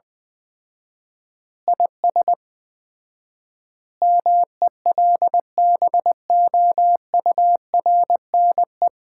Morse code, heard as a keyed tone, about 20 wpm, 700 Hz. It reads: IS MELBOURNE
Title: Common QSO Phrases - 4x Word-Spacing - 20wpm